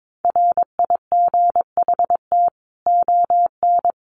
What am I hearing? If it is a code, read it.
Morse code, heard as a keyed tone, about 22 wpm, 700 Hz. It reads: RIGHT ON
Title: A Sound Of Thunder 22wpm